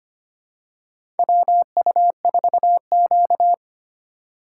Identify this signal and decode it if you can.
Morse code, heard as a keyed tone, about 25 wpm, 700 Hz. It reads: WU4Q
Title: USA Call Signs - 4 Characters - 2x Character Spacing - 25wpm